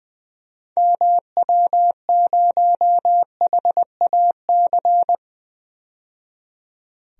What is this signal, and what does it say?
Morse code, heard as a keyed tone, about 21 wpm, 700 Hz. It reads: MW0HAC
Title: DXCC-100 Call Signs - 6 Characters - 20wpm